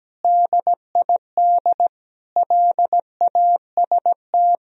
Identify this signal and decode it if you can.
Morse code, heard as a keyed tone, about 17 wpm, 700 Hz. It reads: DID LAST
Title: Sentences from Top 300 Words and Abbreviations 17wpm